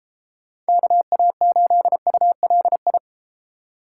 Morse code, heard as a keyed tone, about 31 wpm, 700 Hz. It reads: KA8ULI